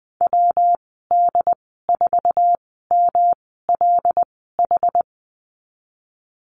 Morse code, heard as a keyed tone, about 20 wpm, spaced as for 15 wpm, 700 Hz. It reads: WD4MLH
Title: 193 Lesson 31g - Letter v - Call Signs 20wpm